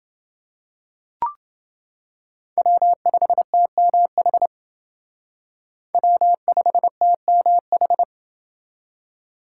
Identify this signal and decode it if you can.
Morse code, heard as a keyed tone, about 29 wpm, 700 Hz. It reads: W5TMH W5TMH